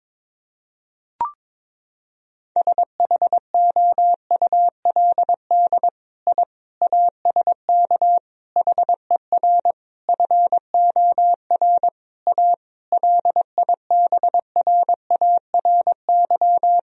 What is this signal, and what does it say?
Morse code, heard as a keyed tone, about 22 wpm, 700 Hz. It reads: SHOULD I ASK HER FOR A LIBRARY
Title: Sentences from Top 1000 Words - Review Part 4 - 22wpm